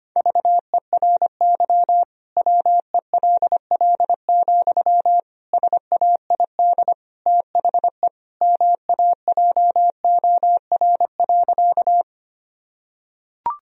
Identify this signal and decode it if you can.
Morse code, heard as a keyed tone, about 25 wpm, 700 Hz. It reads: VERY WELL, SAID THE MAJOR.
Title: Sentences from Top 1000 Words - Review Part 4 - 25wpm